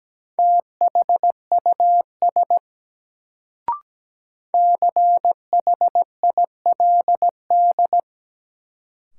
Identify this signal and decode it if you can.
Morse code, heard as a keyed tone, about 17 wpm, 700 Hz. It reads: THUS CHILD
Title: Sentences from Top 800 Words - New Word Drill - 17wpm